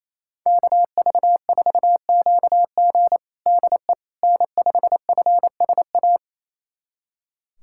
Morse code, heard as a keyed tone, about 27 wpm, 690 Hz. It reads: KV4QG DE N5FSA
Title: USA Callsign DE Callsign - 5 Characters - 28wpm